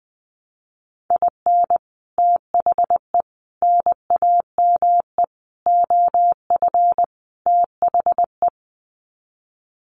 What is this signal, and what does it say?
Morse code, heard as a keyed tone, about 20 wpm, 700 Hz. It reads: IN THE NAME OF THE
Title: Sets of 5 Words Encore 20wpm